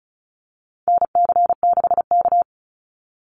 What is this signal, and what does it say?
Morse code, heard as a keyed tone, about 34 wpm, 700 Hz. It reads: NC6K
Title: Call Signs - 35wpm